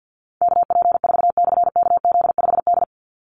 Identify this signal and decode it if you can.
Morse code, heard as a keyed tone, about 50 wpm, 700 Hz. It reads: KP4/KZ5D
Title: Call Signs Repeated 3x - 50wpm